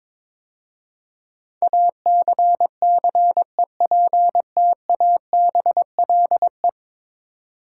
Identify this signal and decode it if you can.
Morse code, heard as a keyed tone, about 22 wpm, 700 Hz. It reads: ACCEPTABLE
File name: practice / 22wpm / 10 Letter Words - First 3 Letters Given - 22wpm